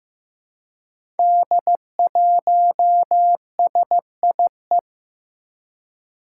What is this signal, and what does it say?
Morse code, heard as a keyed tone, about 15 wpm, 700 Hz. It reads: D1SIE
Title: Five Challenging Letter-Number Combinations 15wpm